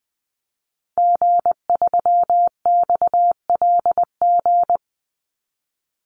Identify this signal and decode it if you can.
Morse code, heard as a keyed tone, about 20 wpm, 700 Hz. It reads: G3XLG
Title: Call Signs - 20wpm